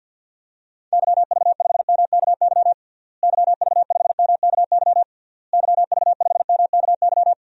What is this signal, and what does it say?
Morse code, heard as a keyed tone, about 51 wpm, 700 Hz. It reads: YV5MKY YV5MKY YV5MKY
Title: DXCC-100 Call Signs - 6 Characters - Repeated 3x - 50wpm